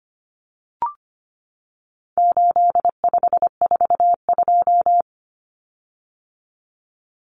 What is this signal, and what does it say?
Morse code, heard as a keyed tone, about 25 wpm, 700 Hz. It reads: 8542
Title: Common Cut Numbers - 4-Digit 25wpm